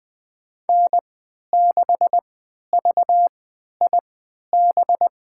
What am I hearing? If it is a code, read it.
Morse code, heard as a keyed tone, about 20 wpm, spaced as for 11 wpm, 700 Hz. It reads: N6VIB
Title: USA Call Signs - 5 Characters - 3x Character Spacing - 20wpm